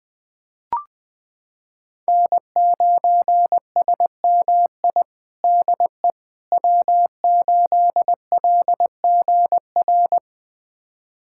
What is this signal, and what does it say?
Morse code, heard as a keyed tone, about 20 wpm, 700 Hz. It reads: N9SMI DE W8LGR
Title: USA Callsign DE Callsign - 5 Characters - 20wpm